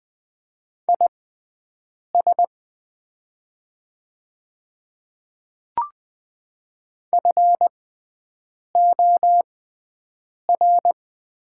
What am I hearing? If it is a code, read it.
Morse code, heard as a keyed tone, about 20 wpm, spaced as for 7 wpm, 700 Hz. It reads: IS FOR
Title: 6x Character Spacing - Top 200 Words - 20wpm